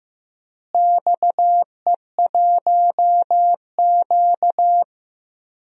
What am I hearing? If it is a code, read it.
Morse code, heard as a keyed tone, about 15 wpm, 700 Hz. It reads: XE1Q
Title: DXCC-100 Call Signs - 4 Characters - Repeated 2x - 15wpm